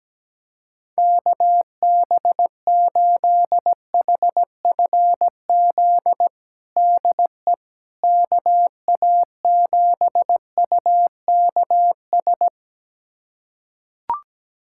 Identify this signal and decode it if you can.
Morse code, heard as a keyed tone, about 17 wpm, 700 Hz. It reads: KB8HFZ DE KA7UKS
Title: USA Callsign DE Callsign - 6 Characters - 17wpm